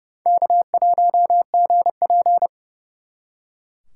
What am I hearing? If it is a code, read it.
Morse code, heard as a keyed tone, about 37 wpm, 700 Hz. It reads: K1GP
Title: Speed Racing - USA 1x2 Call Signs 30wpm